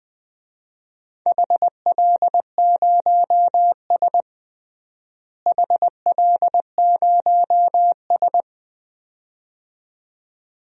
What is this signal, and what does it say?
Morse code, heard as a keyed tone, about 20 wpm, 700 Hz. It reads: HL0S HL0S
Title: DXCC-100 Call Signs - 4 Characters - Repeated 2x - 20wpm